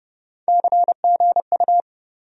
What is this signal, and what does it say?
Morse code, heard as a keyed tone, about 30 wpm, 700 Hz. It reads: CGU
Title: USA Call Sign Postfixes - 3 Letters 30wpm